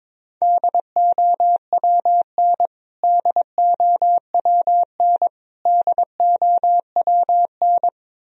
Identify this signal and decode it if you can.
Morse code, heard as a keyed tone, about 23 wpm, 700 Hz. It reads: DOWN DOWN DOWN
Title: Top 100 Words Repeated 22wpm